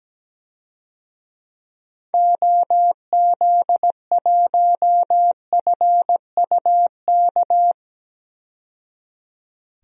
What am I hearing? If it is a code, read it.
Morse code, heard as a keyed tone, about 17 wpm, 700 Hz. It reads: OZ1FUK
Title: DXCC-100 Call Signs - 6 Characters - 17wpm